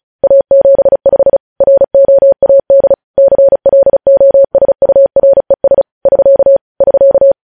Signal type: Morse code